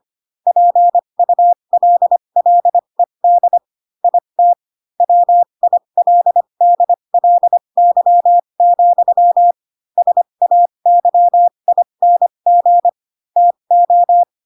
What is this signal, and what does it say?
Morse code, heard as a keyed tone, about 25 wpm, spaced as for 22 wpm, 700 Hz. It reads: PULLED IT WILDLY, SAYING TO
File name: Pinocchio / Pinocchio ch6 18 25 ik1hge